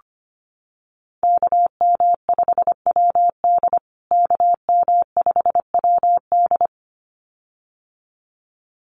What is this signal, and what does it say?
Morse code, heard as a keyed tone, about 25 wpm, 700 Hz. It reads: KM5WD KM5WD
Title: USA Call Signs Repeated 2x 25wpm